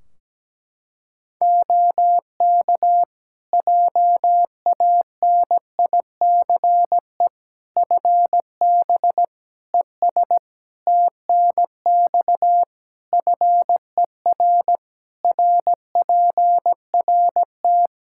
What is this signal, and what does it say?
Morse code, heard as a keyed tone, about 17 wpm, 700 Hz. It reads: OK JANICE FB ES TNX FER RPRT
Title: K1USN SST QSO Practice 17wpm